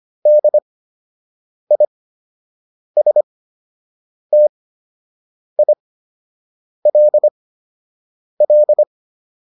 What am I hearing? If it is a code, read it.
Morse code, heard as a keyed tone, about 25 wpm, spaced as for 7 wpm, 600 Hz. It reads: DISTILL
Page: fundamental-practice-files